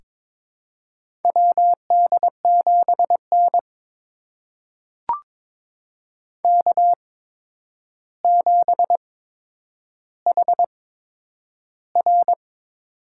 Morse code, heard as a keyed tone, about 22 wpm, spaced as for 9 wpm, 700 Hz. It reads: WD7N K7HR
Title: USA Call Signs - 4 Characters - 8x Character Spacing - 22wpm